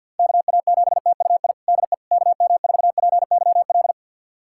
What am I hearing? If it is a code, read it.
Morse code, heard as a keyed tone, about 48 wpm, 700 Hz. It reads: KA7TUI DE KM4PYL
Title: USA Callsign DE Callsign 50wpm